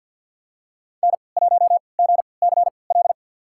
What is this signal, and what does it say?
Morse code, heard as a keyed tone, about 51 wpm, spaced as for 28 wpm, 700 Hz. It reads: N1GCL
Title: USA Call Signs - 5 Characters - 3x Character Spacing - 50wpm